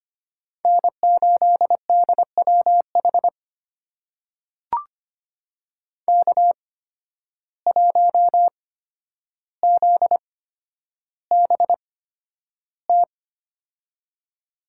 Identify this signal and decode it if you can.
Morse code, heard as a keyed tone, about 25 wpm, spaced as for 11 wpm, 700 Hz. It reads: N8DWH K1ZBT
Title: USA Call Signs - 5 Characters - 8x Character Spacing - 25wpm